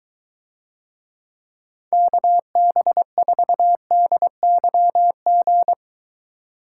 Morse code, heard as a keyed tone, about 23 wpm, 700 Hz. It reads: KB4DYG